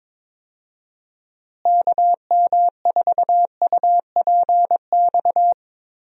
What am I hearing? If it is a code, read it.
Morse code, heard as a keyed tone, about 22 wpm, 700 Hz. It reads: KM4UPX
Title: USA Call Signs - 6 Characters - Repeated 2x - 22wpm